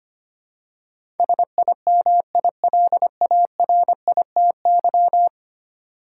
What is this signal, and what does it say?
Morse code, heard as a keyed tone, about 25 wpm, 700 Hz. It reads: SIMILARITY
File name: 10 Letter Words - First 3 Letters Given - 25wpm